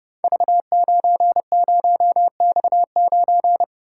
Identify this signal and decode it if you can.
Morse code, heard as a keyed tone, about 30 wpm, 700 Hz. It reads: V90X9
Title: Code Groups - 5 Letter:Numbers - 30wpm